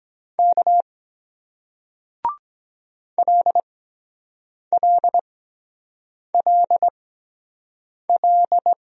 Morse code, heard as a keyed tone, about 23 wpm, spaced as for 7 wpm, 700 Hz. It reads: K LLLL